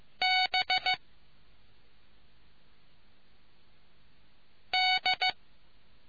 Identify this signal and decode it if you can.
Morse code, heard as a keyed tone, about 15 wpm, 740 Hz. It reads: B D